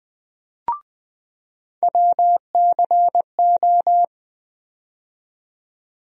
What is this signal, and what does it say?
Morse code, heard as a keyed tone, about 20 wpm, 700 Hz. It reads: WCO
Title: USA Call Sign Postfixes - 3 Letters 20wpm